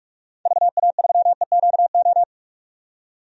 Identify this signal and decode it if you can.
Morse code, heard as a keyed tone, about 41 wpm, 700 Hz. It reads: VA3EQO
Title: DXCC-100 Call Signs - 6 Characters - 45wpm